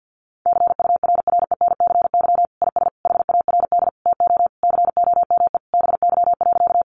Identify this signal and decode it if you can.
Morse code, heard as a keyed tone, about 48 wpm, 700 Hz. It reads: CURRENCY IS HARD TO COME BY.
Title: Sentences from Top 1500 Words 50wpm